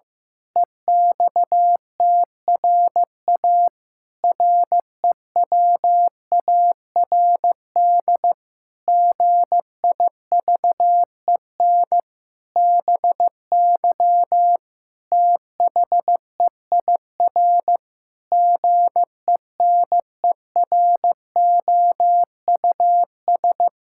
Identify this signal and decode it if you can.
Morse code, heard as a keyed tone, about 15 wpm, 700 Hz. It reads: EXTRA REWARD GIVEN BY THEIR GENEROUS
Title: Pangram Challenge v2 - 15wpm